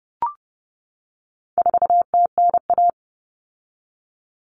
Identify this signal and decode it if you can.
Morse code, heard as a keyed tone, about 30 wpm, 700 Hz. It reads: 4TNA
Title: Common Cut Numbers - 4-Digit 30wpm